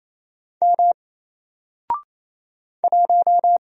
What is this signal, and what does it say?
Morse code, heard as a keyed tone, about 27 wpm, 700 Hz. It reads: M 1
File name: Single-Letter-Number-28wpm